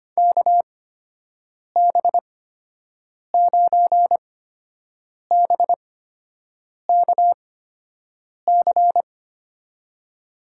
Morse code, heard as a keyed tone, about 25 wpm, spaced as for 7 wpm, 700 Hz. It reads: KB9BKC
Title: USA Call Signs - 6 Characters - 8x Character Spacing - 25wpm